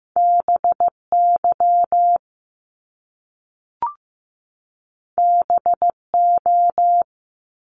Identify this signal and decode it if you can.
Morse code, heard as a keyed tone, about 15 wpm, 700 Hz. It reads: BY BO
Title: Top Two Letter Combinations in Words 15wpm